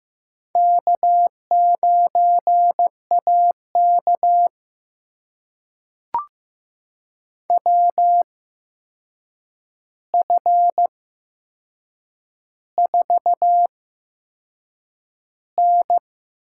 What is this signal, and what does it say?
Morse code, heard as a keyed tone, about 15 wpm, spaced as for 6 wpm, 700 Hz. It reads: K9AK WF4N
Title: USA Call Signs - 4 Characters - 8x Character Spacing - 15wpm